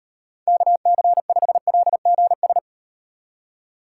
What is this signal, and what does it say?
Morse code, heard as a keyed tone, about 38 wpm, 700 Hz. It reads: KC5LGS